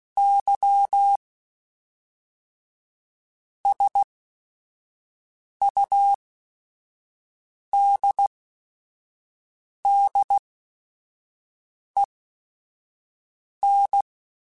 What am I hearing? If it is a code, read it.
Morse code, heard as a keyed tone, about 16 wpm, spaced as for 5 wpm, 800 Hz. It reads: Y SUDDEN
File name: Technical Subjects / Technical Subjects-mp3s / Reactance @5